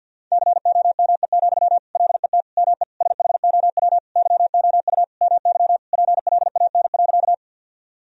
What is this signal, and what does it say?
Morse code, heard as a keyed tone, about 52 wpm, 700 Hz. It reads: COME, LET ME SHOW YOU MY PLAN.
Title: Sentences from Top 1000 Words - Review Part 2 - 50wpm